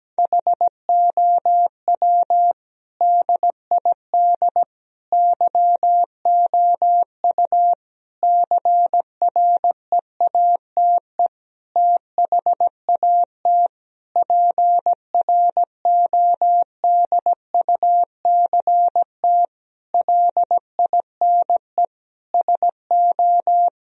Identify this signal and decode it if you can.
Morse code, heard as a keyed tone, about 17 wpm, 700 Hz. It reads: HOW DID YOU CREATE THAT PRODUCT LINE SO
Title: Sentences from Top 1000 Words - Review Part 3 - 17wpm